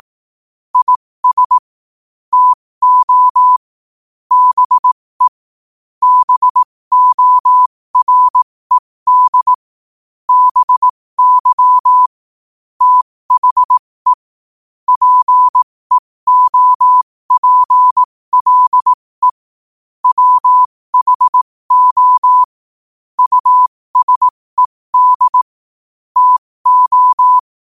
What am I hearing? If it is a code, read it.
Morse code, heard as a keyed tone, about 18 wpm, spaced as for 15 wpm, 1000 Hz. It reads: IS TO BE BORED BY THE PEOPLE WHO USED TO
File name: qotdfiles / 15wpm